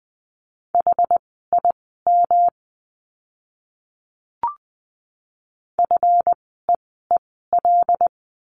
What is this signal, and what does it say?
Morse code, heard as a keyed tone, about 20 wpm, spaced as for 15 wpm, 700 Hz. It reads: HIM FEEL
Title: 139 Lesson 23f - Letter f - Words 20wpm